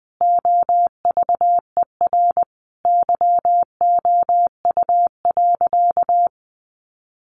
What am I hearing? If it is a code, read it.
Morse code, heard as a keyed tone, about 20 wpm, 700 Hz. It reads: OVER YOU.
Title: Sentences from Top 900 Words 20wpm